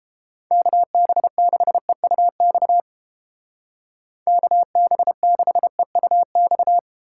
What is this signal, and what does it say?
Morse code, heard as a keyed tone, about 30 wpm, 700 Hz. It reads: KB6EUX KB6EUX